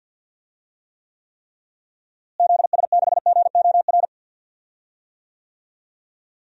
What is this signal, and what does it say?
Morse code, heard as a keyed tone, about 50 wpm, 700 Hz. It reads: ZS6GOR
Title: DXCC-100 Call Signs - 6 Characters - Repeated 3x - 50wpm